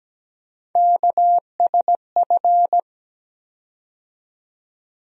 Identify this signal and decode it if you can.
Morse code, heard as a keyed tone, about 17 wpm, 700 Hz. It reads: KSF
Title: USA Call Sign Postfixes - 3 Letters 17wpm